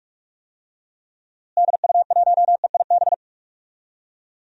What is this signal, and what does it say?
Morse code, heard as a keyed tone, about 42 wpm, 700 Hz. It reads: DU1EIB
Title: DXCC-100 Call Signs - 6 Characters - Repeated 3x - 45wpm